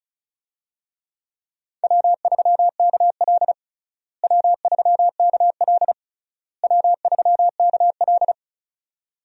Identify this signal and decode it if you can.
Morse code, heard as a keyed tone, about 34 wpm, 700 Hz. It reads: W3KL W3KL W3KL
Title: USA Call Signs - 4 Characters - Repeated 3x - 35wpm